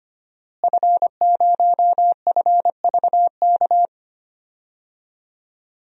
Morse code, heard as a keyed tone, about 25 wpm, 700 Hz. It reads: F0FVK
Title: DXCC-100 Call Signs - 5 Characters - Repeated 3x - 25wpm